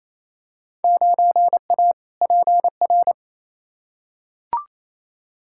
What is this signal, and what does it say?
Morse code, heard as a keyed tone, about 28 wpm, 700 Hz. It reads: 9A PR
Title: Field Day Exchanges Encore - 28wpm